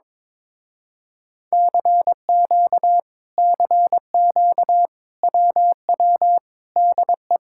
Morse code, heard as a keyed tone, about 22 wpm, 700 Hz. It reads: CQ CQ WW DE
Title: Speed-Racing Common QSO Phrases - 22wpm